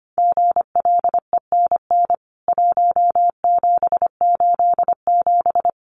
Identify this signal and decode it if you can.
Morse code, heard as a keyed tone, about 25 wpm, 700 Hz. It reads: GLENN 1787
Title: CWT Exchange 25wpm